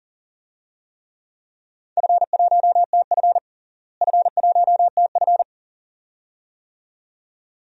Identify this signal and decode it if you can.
Morse code, heard as a keyed tone, about 40 wpm, 700 Hz. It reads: F1TF F1TF
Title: DXCC-100 Call Signs - 4 Characters - Repeated 2x - 40wpm